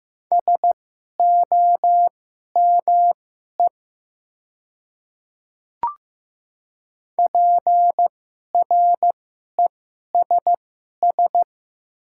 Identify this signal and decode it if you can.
Morse code, heard as a keyed tone, about 15 wpm, spaced as for 11 wpm, 690 Hz. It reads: SOME PRESS
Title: 2x Character Spacing - Top 300 Words - 15wpm